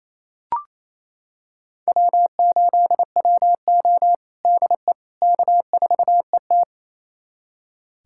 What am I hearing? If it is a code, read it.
Morse code, heard as a keyed tone, about 28 wpm, 700 Hz. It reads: W8WO DE K4ET
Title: USA Callsign DE Callsign - 4 Characters - 28wpm